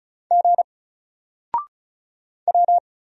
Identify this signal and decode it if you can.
Morse code, heard as a keyed tone, about 35 wpm, 700 Hz. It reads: G W
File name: Single-Letter-Number-35wpm